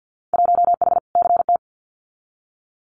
Morse code, heard as a keyed tone, about 50 wpm, 700 Hz. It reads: 1H CT
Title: Winter Field Day Exchanges 50wpm